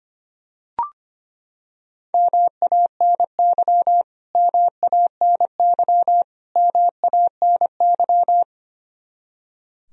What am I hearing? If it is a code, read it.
Morse code, heard as a keyed tone, about 25 wpm, 700 Hz. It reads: MANY MANY MANY
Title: Top 100 Words Repeated 25wpm